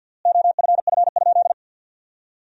Morse code, heard as a keyed tone, about 53 wpm, 700 Hz. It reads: OFF?
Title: 3x Word-Spacing - Sentences from Top 500 Words - 50wpm